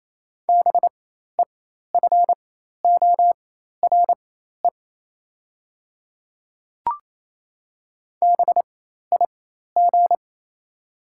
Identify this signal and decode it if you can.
Morse code, heard as a keyed tone, about 28 wpm, spaced as for 13 wpm, 700 Hz. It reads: BEFORE BIG
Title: 4x Character Spacing - Top 300 Words - 28wpm